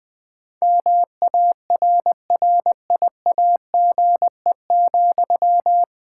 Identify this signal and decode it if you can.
Morse code, heard as a keyed tone, about 20 wpm, 700 Hz. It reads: MARRIAGE,
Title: Sentences from Top 1000 Words - Review Part 5 - 20wpm